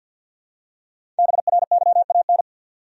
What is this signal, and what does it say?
Morse code, heard as a keyed tone, about 52 wpm, 700 Hz. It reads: BRYAN
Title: USA Common Names - Max 5 Letters - 50wpm